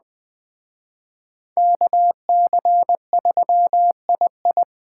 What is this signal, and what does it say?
Morse code, heard as a keyed tone, about 20 wpm, 700 Hz. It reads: KC3II